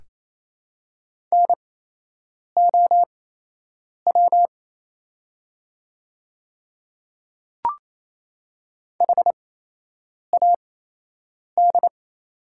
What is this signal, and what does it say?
Morse code, heard as a keyed tone, about 28 wpm, spaced as for 8 wpm, 700 Hz. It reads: NOW HAD
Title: 8x Character Spacing - Top 100 Words - 28wpm